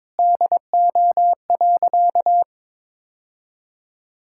Morse code, heard as a keyed tone, about 22 wpm, 700 Hz. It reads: DO.
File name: practice / 22wpm / Sentences with 3-Word Phrases - 22wpm